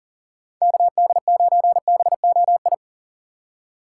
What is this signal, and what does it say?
Morse code, heard as a keyed tone, about 40 wpm, 700 Hz. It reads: KD9BOI
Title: USA Call Signs - 4x Character Spacing - 40wpm